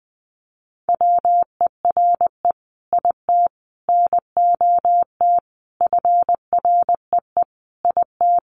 Morse code, heard as a keyed tone, about 20 wpm, 700 Hz. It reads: WERE IT NOT FREE IT
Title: Sentences from Top 900 Words 20wpm